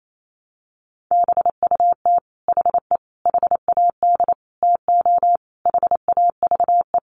Morse code, heard as a keyed tone, about 28 wpm, 700 Hz. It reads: BUT HE HAD TO HAVE
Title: Sentences from Top 100 Words 28wpm